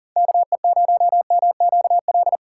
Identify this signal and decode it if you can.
Morse code, heard as a keyed tone, about 40 wpm, 700 Hz. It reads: KE0MQL
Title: USA Call Signs - 6 Characters - 4x Character Spacing - 40wpm